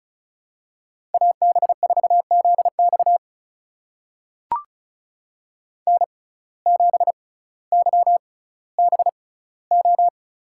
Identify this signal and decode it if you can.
Morse code, heard as a keyed tone, about 36 wpm, spaced as for 18 wpm, 700 Hz. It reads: AB4ZX N7YBO
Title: USA Call Signs - 5 Characters - 6x Character Spacing - 35wpm